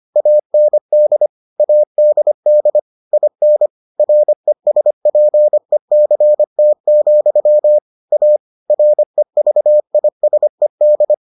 Morse code, heard as a keyed tone, about 25 wpm, 600 Hz. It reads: AND ADD IN RESPECT, A REVISED